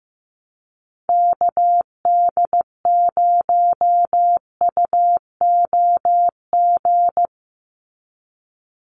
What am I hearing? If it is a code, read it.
Morse code, heard as a keyed tone, about 15 wpm, 700 Hz. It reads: KD0UOG
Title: USA Call Signs 15wpm